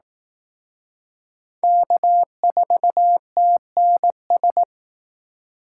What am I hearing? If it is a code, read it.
Morse code, heard as a keyed tone, about 18 wpm, 700 Hz. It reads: K4TNS
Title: Speed Racing - USA 1x3 Call Signs 15wpm